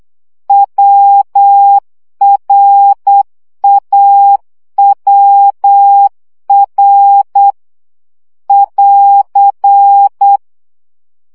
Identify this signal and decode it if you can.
Morse code, heard as a keyed tone, about 8 wpm, 800 Hz. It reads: WRAWR <AR>
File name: CW gyakorlas a r w 7 baud